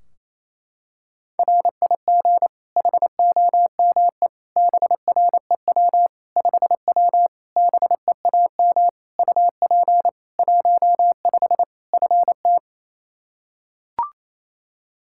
Morse code, heard as a keyed tone, about 27 wpm, 700 Hz. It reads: RIG HOME BREW 5W BEAM UP 15 FT
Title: K1USN SST QSO Practice 28wpm